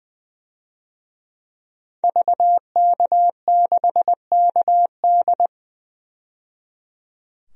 Morse code, heard as a keyed tone, about 21 wpm, 700 Hz. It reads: VK6KD
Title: DXCC-100 Call Signs - 5 Characters - 20wpm